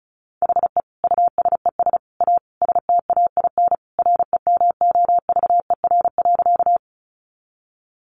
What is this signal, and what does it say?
Morse code, heard as a keyed tone, about 34 wpm, 700 Hz. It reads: HE USES A STAIN REMOVER.
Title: Additional Easy Phrases 35wpm